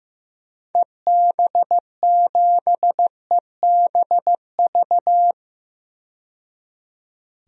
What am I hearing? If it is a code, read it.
Morse code, heard as a keyed tone, about 15 wpm, 700 Hz. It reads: EB7EBV
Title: DXCC-100 Call Signs - 6 Characters - 15wpm